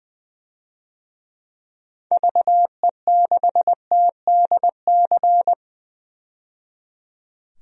Morse code, heard as a keyed tone, about 21 wpm, 700 Hz. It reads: VE6TDC
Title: DXCC-100 Call Signs - 6 Characters - 20wpm